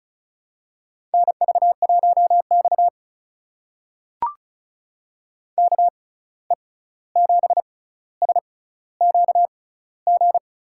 Morse code, heard as a keyed tone, about 36 wpm, spaced as for 16 wpm, 700 Hz. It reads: NV1X KE7SQG
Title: USA Call Signs - 6x Character Spacing - 35wpm